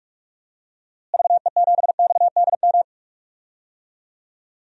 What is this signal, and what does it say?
Morse code, heard as a keyed tone, about 46 wpm, 700 Hz. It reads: VE7XDM
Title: DXCC-100 Call Signs - 6 Characters - Repeated 3x - 45wpm